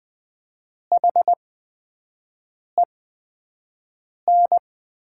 Morse code, heard as a keyed tone, about 20 wpm, spaced as for 5 wpm, 700 Hz. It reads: HEN
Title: 8x Character Spacing - Top 100 Words - 20wpm